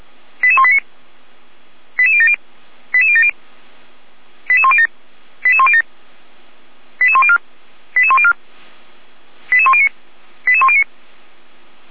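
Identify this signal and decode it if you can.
Five-tone selective call (ZVEI), heard as quick runs of five tones, tones of about 70 ms. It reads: 88189 88070 88070 88107 88107 88104 88104 88109 88109
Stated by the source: ZVEI five-tone selective call 88127 received via SDR for the Feuerwehr Orlamünde fire brigade.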